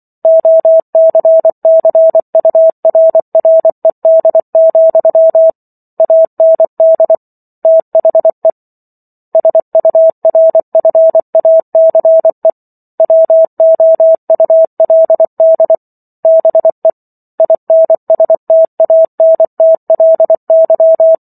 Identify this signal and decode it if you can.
Morse code, heard as a keyed tone, about 24 wpm, 650 Hz. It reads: OCCURRED, AND THE SURFACE WOULD BE INSTANTLY